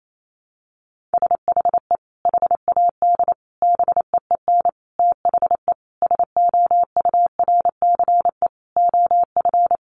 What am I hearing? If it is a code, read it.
Morse code, heard as a keyed tone, about 28 wpm, 700 Hz. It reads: SHE HAD BEEN THE SOURCE OF
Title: Sentences from Top 1200 Words 28wpm